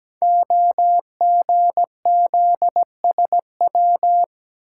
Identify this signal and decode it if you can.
Morse code, heard as a keyed tone, about 17 wpm, 700 Hz. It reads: OGZSW
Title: Code Groups - 5 Letter:Numbers - 17wpm